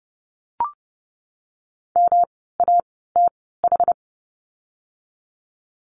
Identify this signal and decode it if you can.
Morse code, heard as a keyed tone, about 30 wpm, spaced as for 17 wpm, 700 Hz. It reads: MATH
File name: course / 30:10wpm / 108 Lesson 18f - Letter m - Words 30wpm